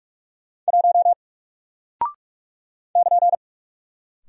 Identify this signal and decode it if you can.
Morse code, heard as a keyed tone, about 46 wpm, 700 Hz. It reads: 1 <KN>
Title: Single Character (Letter:Number:Punctuation:Prosign) 45wpm